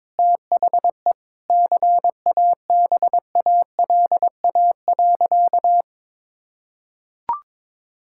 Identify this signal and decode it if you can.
Morse code, heard as a keyed tone, about 22 wpm, 700 Hz. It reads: THE CABALA.
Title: Pangram Challenge v2 - 22wpm